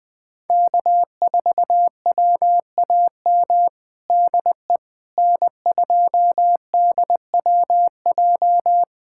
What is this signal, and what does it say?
Morse code, heard as a keyed tone, about 21 wpm, 700 Hz. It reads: K4WAM DE N2DWJ
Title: USA Callsign DE Callsign - 5 Characters - 20wpm